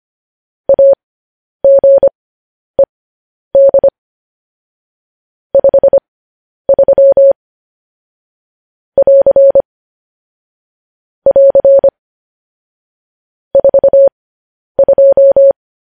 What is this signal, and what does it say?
Morse code, heard as a keyed tone, about 25 wpm, spaced as for 10 wpm, 550 Hz. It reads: AGED 53 <AR> <AR> 42